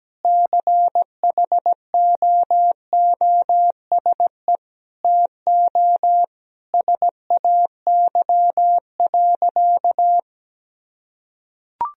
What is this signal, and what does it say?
Morse code, heard as a keyed tone, about 17 wpm, 700 Hz. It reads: CHOOSE TO SAY.
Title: Sentences from Top 1000 Words - Review Part 4 - 17wpm